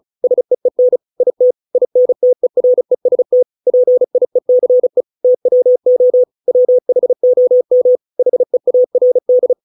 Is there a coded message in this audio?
Yes: Morse code from 0.2 to 9.5 s, heard as a keyed tone, about 34 wpm, 500 Hz. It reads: SEEN IT INTEREST PIECE TWO WHOM HEARD